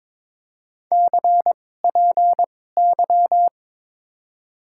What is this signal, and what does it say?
Morse code, heard as a keyed tone, about 21 wpm, spaced as for 16 wpm, 700 Hz. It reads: CPY
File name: Top 90 QSO Elements - 2x Character Spacing - 22wpm